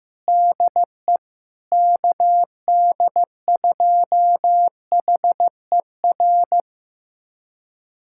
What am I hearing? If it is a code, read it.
Morse code, heard as a keyed tone, about 15 wpm, 700 Hz. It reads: DE KD2HER
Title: USA Callsign DE Callsign - 6 Characters - 15wpm